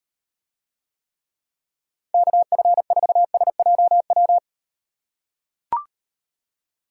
Morse code, heard as a keyed tone, about 38 wpm, 700 Hz. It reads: KF4SJW